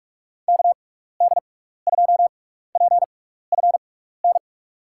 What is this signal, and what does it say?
Morse code, heard as a keyed tone, about 44 wpm, spaced as for 15 wpm, 700 Hz. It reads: KD2PFN
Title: USA Call Signs - 6 Characters - 6x Character Spacing - 45wpm